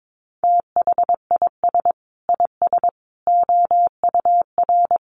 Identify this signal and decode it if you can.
Morse code, heard as a keyed tone, about 22 wpm, 700 Hz. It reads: THIS IS OUR
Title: Sentences from Top 1000 Words 22wpm